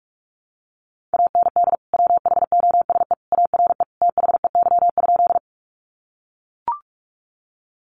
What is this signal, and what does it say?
Morse code, heard as a keyed tone, about 44 wpm, 700 Hz. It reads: AND WHOSE ARE THEY?
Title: Sentences from Top 1200 Words 45wpm